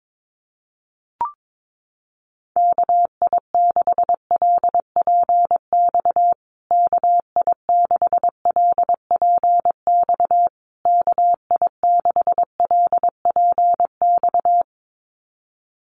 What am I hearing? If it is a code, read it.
Morse code, heard as a keyed tone, about 22 wpm, 700 Hz. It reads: KI6LPX KI6LPX KI6LPX
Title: USA Call Signs Repeated 3x 22wpm